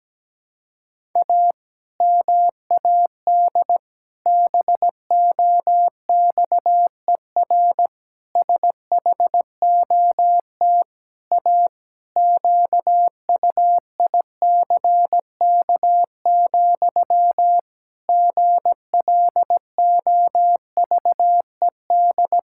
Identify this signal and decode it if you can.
Morse code, heard as a keyed tone, about 17 wpm, 700 Hz. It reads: A MAD BOXER SHOT A QUICK, GLOVED
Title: Pangram Challenge v2 - 17wpm